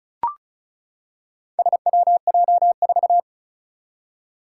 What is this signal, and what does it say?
Morse code, heard as a keyed tone, about 36 wpm, 700 Hz. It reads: SWJ4
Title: Four Challenging Letter-Number Combinations 35wpm